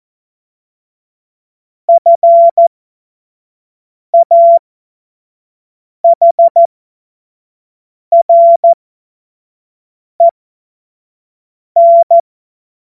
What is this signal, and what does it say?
Morse code, heard as a keyed tone, about 14 wpm, spaced as for 5 wpm, 680 Hz. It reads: FAHREN
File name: Morsekurs / Witze / Witz 16 Januar 2024 25 F70